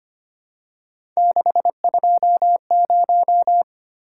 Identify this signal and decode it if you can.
Morse code, heard as a keyed tone, about 25 wpm, 700 Hz. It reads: 620
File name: Three Letter-Number Combinations - 25wpm